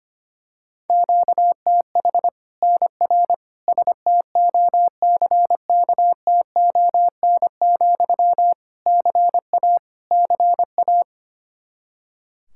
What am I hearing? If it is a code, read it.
Morse code, heard as a keyed tone, about 25 wpm, 700 Hz. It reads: QTH NR STOCKTON, CA CA
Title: Practice QSO - Exchange 1 - Beginning - 25wpm